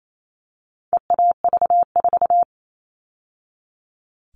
Morse code, heard as a keyed tone, about 28 wpm, 700 Hz. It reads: EAV4
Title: All Cut Numbers - 4-Digit 28wpm